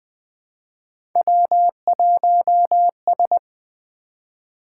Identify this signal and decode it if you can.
Morse code, heard as a keyed tone, about 20 wpm, 700 Hz. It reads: W1S
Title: Three Letter-Number Combinations - 20wpm